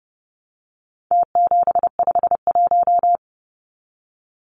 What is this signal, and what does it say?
Morse code, heard as a keyed tone, about 30 wpm, 700 Hz. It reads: T751
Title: Numbers - Contest Style - 4-Digit 30wpm